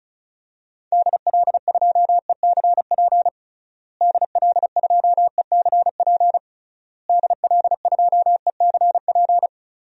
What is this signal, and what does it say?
Morse code, heard as a keyed tone, about 36 wpm, 700 Hz. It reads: DL2ECP DL2ECP DL2ECP
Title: DXCC-100 Call Signs - 6 Characters - Repeated 3x - 35wpm